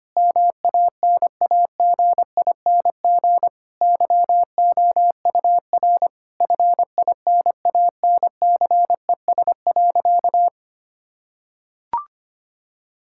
Morse code, heard as a keyed tone, about 25 wpm, 700 Hz. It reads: MANAGING YOUR FINANCES.
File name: practice / 25wpm / Difficult Sentences with Hints - 25wpm